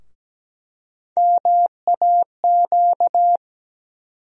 Morse code, heard as a keyed tone, about 17 wpm, 700 Hz. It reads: MAQ
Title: USA Call Sign Postfixes - 3 Letters 17wpm